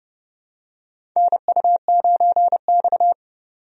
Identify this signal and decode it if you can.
Morse code, heard as a keyed tone, about 30 wpm, 700 Hz. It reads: NU9X
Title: USA Call Signs - 4 Characters - 30wpm